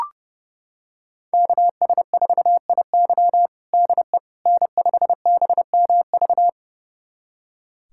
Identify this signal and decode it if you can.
Morse code, heard as a keyed tone, about 30 wpm, 700 Hz. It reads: KS4IY DE N5BMV
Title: USA Callsign DE Callsign - 5 Characters - 30wpm